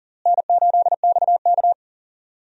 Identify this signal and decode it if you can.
Morse code, heard as a keyed tone, about 40 wpm, 700 Hz. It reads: N8XK
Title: USA Call Signs - 4 Characters - Repeated 3x - 40wpm